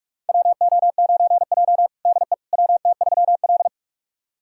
Morse code, heard as a keyed tone, about 45 wpm, 700 Hz. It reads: WO9J DE WT3L
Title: USA Callsign DE Callsign - 4 Characters - 45wpm